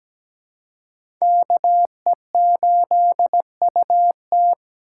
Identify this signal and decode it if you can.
Morse code, heard as a keyed tone, about 17 wpm, 700 Hz. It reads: KE8UT
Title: USA Call Signs - 5 Characters - 2x Character Spacing - 17wpm